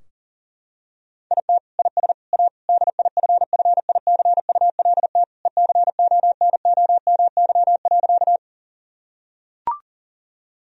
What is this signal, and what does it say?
Morse code, heard as a keyed tone, about 40 wpm, 700 Hz. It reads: IT IS A DIFFICULT ECONOMY.
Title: Sentences from Top 1000 Words - Review Part 3 - 40wpm